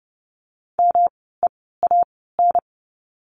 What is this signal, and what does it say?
Morse code, heard as a keyed tone, about 30 wpm, spaced as for 17 wpm, 700 Hz. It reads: MEAN
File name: course / 30:10wpm / 230 Lesson 37f - Letter x - Words 30wpm